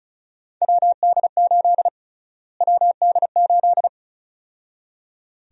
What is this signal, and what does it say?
Morse code, heard as a keyed tone, about 36 wpm, 700 Hz. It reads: WD8 WD8
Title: USA Call Sign Prefixes - Repeated 2x - 35wpm